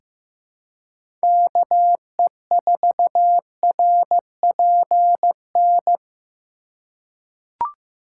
Morse code, heard as a keyed tone, about 15 wpm, 700 Hz. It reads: KE4RPN
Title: USA Call Signs - 3x Character Spacing - 15wpm